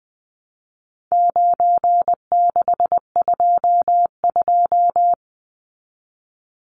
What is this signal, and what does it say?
Morse code, heard as a keyed tone, about 20 wpm, 700 Hz. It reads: 9622
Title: Numbers - 4-Digit 20wpm